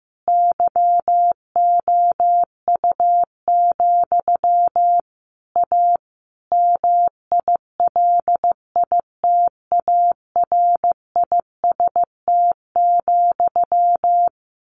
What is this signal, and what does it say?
Morse code, heard as a keyed tone, about 15 wpm, 700 Hz. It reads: YOU, A MILITARIST,
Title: A Sound Of Thunder 15wpm